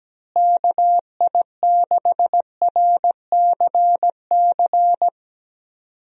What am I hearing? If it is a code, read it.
Morse code, heard as a keyed tone, about 17 wpm, 700 Hz. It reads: KI6RCC
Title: USA Call Signs - 6 Characters - Repeated 2x - 17wpm